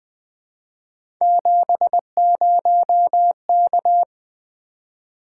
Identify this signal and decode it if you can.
Morse code, heard as a keyed tone, about 20 wpm, 700 Hz. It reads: 70K
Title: Three Letter-Number Combinations - 20wpm